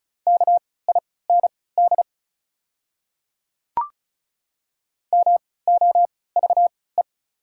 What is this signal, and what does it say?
Morse code, heard as a keyed tone, about 36 wpm, spaced as for 20 wpm, 690 Hz. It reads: KIND MOVE
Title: 3x Character Spacing - Top 200 Words - 35wpm